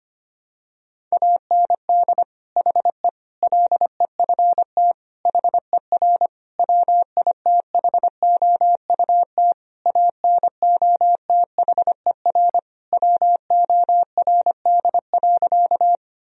Morse code, heard as a keyed tone, about 25 wpm, 700 Hz. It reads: AND HE LEFT HER WITHOUT ANOTHER WORD.
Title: Sentences from Top 1000 Words - Review Part 3 - 25wpm